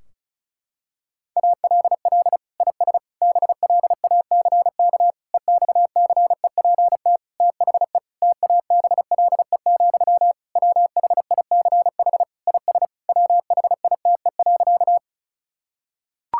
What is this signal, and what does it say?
Morse code, heard as a keyed tone, about 36 wpm, 700 Hz. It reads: ALL IS BLACK EXCEPT THE TABLE, WHICH IS WHITE.
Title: Sentences from Top 1600 Words 35wpm